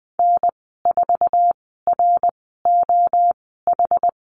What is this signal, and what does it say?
Morse code, heard as a keyed tone, about 20 wpm, 700 Hz. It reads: N 4 R O H
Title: 062 Lesson 11g - Letter h - Call Signs 20wpm